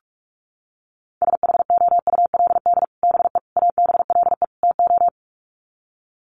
Morse code, heard as a keyed tone, about 44 wpm, 700 Hz. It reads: SHOULD BE ABLE TO
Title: Sets of 4 Words 45wpm